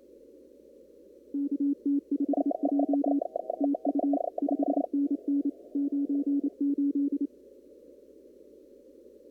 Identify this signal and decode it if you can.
Morse code, heard as a keyed tone, about 28 wpm, 280 Hz. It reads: KT5J TU 5NN 98
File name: CW / kt5j wpx cw-op-k5pi